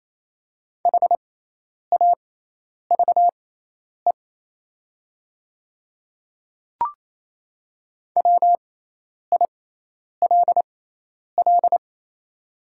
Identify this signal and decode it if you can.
Morse code, heard as a keyed tone, about 28 wpm, spaced as for 10 wpm, 700 Hz. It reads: HAVE WILL
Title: 6x Character Spacing - Top 200 Words - 28wpm